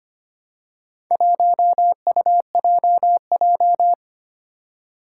Morse code, heard as a keyed tone, about 25 wpm, 700 Hz. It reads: 1UJJ
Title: Four Challenging Letter-Number Combinations 25wpm